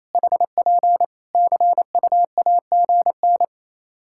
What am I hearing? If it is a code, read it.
Morse code, heard as a keyed tone, about 27 wpm, 700 Hz. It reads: HP CUAGN
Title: Practice QSO - Exchange 4 - Ending - 28wpm